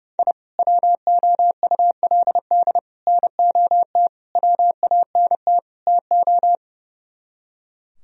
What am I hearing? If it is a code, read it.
Morse code, heard as a keyed tone, about 36 wpm, 700 Hz. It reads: I WOULD NOT WANT TO
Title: Speed-Racing - Sets of 5 Words 30wpm